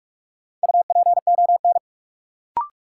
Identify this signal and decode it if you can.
Morse code, heard as a keyed tone, about 46 wpm, 700 Hz. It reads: UPON
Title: Sentences from Top 1100 Words - New Word Drill Alt - 45wpm